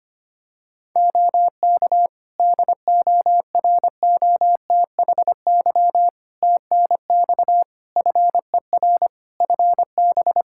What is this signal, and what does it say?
Morse code, heard as a keyed tone, about 25 wpm, 700 Hz. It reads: OK DOROTHY TNX FER FB
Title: Practice QSO - Exchange 4 - Ending - 25wpm